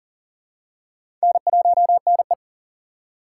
Morse code, heard as a keyed tone, about 40 wpm, 700 Hz. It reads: N1NE
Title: USA Call Signs - 4 Characters - 4x Character Spacing - 40wpm